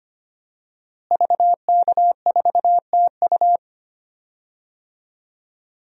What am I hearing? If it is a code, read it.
Morse code, heard as a keyed tone, about 25 wpm, 700 Hz. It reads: VK4TU
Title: DXCC-100 Call Signs - 5 Characters - Repeated 3x - 25wpm